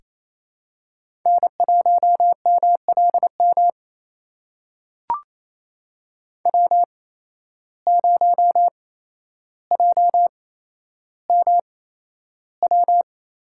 Morse code, heard as a keyed tone, about 28 wpm, spaced as for 12 wpm, 700 Hz. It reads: N1MLM W0JMW
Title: USA Call Signs - 5 Characters - 8x Character Spacing - 28wpm